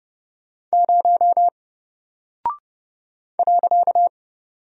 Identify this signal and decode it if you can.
Morse code, heard as a keyed tone, about 30 wpm, 700 Hz. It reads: 0 .
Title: Single Character (Letter:Number:Punctuation:Prosign) Alt 30wpm